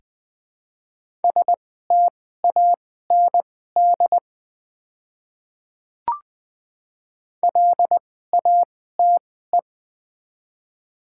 Morse code, heard as a keyed tone, about 20 wpm, spaced as for 14 wpm, 700 Hz. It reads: STAND LATE
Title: 2x Character Spacing - Top 300 Words - 20wpm